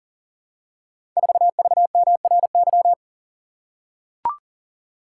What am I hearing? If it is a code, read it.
Morse code, heard as a keyed tone, about 42 wpm, 700 Hz. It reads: 4VMRY
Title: Code Groups - 5 Letter:Numbers - 40wpm